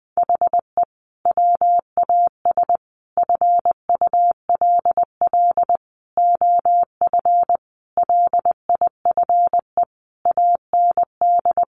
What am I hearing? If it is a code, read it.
Morse code, heard as a keyed tone, about 20 wpm, 700 Hz. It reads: HE WAS FULL OF LIFE AND
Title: Sentences from Top 1100 Words 20wpm